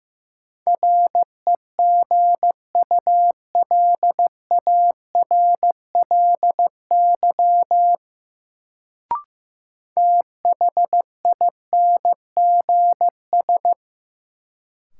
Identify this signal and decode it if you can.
Morse code, heard as a keyed tone, about 15 wpm, 700 Hz. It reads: REGULARLY THINGS
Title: Sentences from Top 1400 Words - New Word Drill Alt - 15wpm